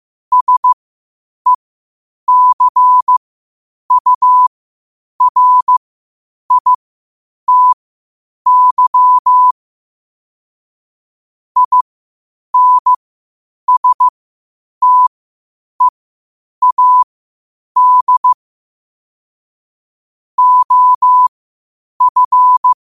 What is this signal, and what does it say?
Morse code, heard as a keyed tone, about 15 wpm, spaced as for 8 wpm, 1000 Hz. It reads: SECURITY INSTEAD OF
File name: qotdfiles / 8wpm